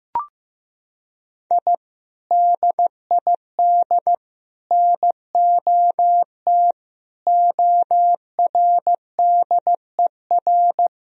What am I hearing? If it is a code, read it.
Morse code, heard as a keyed tone, about 15 wpm, 700 Hz. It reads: I DID NOT ORDER
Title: Sentences from Top 1000 Words - Review Part 2 - 15wpm